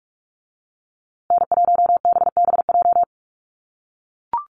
Morse code, heard as a keyed tone, about 44 wpm, 700 Hz. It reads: N1BBJ
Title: USA Call Signs Repeated 3x 45wpm